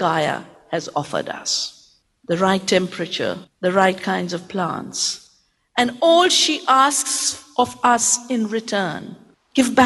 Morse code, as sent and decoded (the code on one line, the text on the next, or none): none